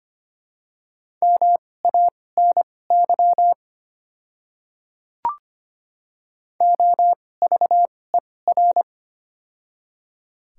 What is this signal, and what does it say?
Morse code, heard as a keyed tone, about 25 wpm, spaced as for 18 wpm, 700 Hz. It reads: MANY OVER
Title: 2x Character Spacing - Top 100 Words - 25wpm